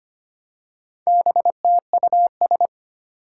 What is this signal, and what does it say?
Morse code, heard as a keyed tone, about 25 wpm, 700 Hz. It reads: BTUS
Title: Four Challenging Letter-Number Combinations 25wpm